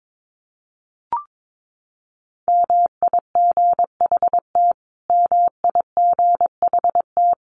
Morse code, heard as a keyed tone, about 22 wpm, 700 Hz. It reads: MIGHT MIGHT
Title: Top 500 Words Repeated 22wpm